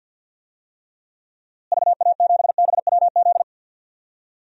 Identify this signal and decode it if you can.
Morse code, heard as a keyed tone, about 53 wpm, 700 Hz. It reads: VA7BWZ
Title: DXCC-100 Call Signs - 6 Characters - Repeated 2x - 50wpm